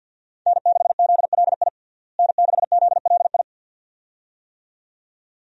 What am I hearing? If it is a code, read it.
Morse code, heard as a keyed tone, about 51 wpm, 700 Hz. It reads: N6ZLI N6ZLI
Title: USA Call Signs - 5 Characters - Repeated 2x - 50wpm